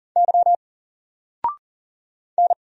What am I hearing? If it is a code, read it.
Morse code, heard as a keyed tone, about 40 wpm, 700 Hz. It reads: Y N
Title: Single Letter-Number Alt 40wpm